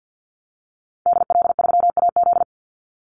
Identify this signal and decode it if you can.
Morse code, heard as a keyed tone, about 50 wpm, 700 Hz. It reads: DL3AZ
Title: Call Signs - 50wpm